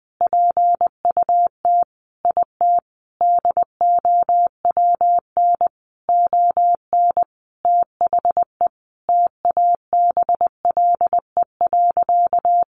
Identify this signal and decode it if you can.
Morse code, heard as a keyed tone, about 20 wpm, 700 Hz. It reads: PUT IT DOWN ON THE TABLE.
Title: Sentences from Top 1100 Words 20wpm